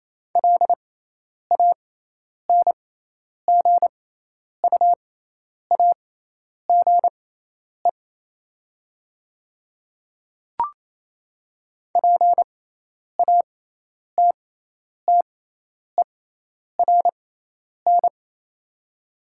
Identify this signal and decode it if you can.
Morse code, heard as a keyed tone, about 28 wpm, spaced as for 10 wpm, 700 Hz. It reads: LANGUAGE PATTERN
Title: 6x Character Spacing - Top 500 Words - 28wpm